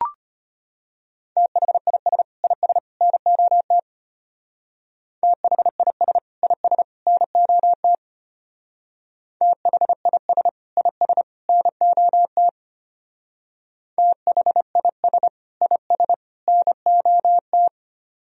Morse code, heard as a keyed tone, about 31 wpm, 700 Hz. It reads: THIS IS NOT THIS IS NOT THIS IS NOT THIS IS NOT